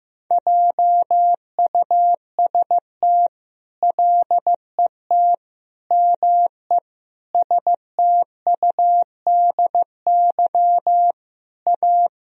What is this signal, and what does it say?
Morse code, heard as a keyed tone, about 15 wpm, 700 Hz. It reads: JUST LET ME STUDY A
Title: Sentences from Top 1000 Words - Review Part 1 - 15wpm